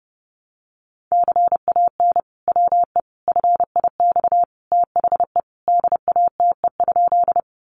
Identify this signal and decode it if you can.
Morse code, heard as a keyed tone, about 30 wpm, 700 Hz. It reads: CAN WE FIX THE DATE?
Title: Sentences from Top 1400 Words 30wpm